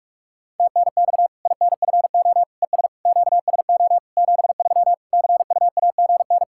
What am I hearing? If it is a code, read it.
Morse code, heard as a keyed tone, about 44 wpm, 700 Hz. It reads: TNX INFO ES QSO 73 CUAGN
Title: Practice QSO - Exchange 4 - Ending - 45wpm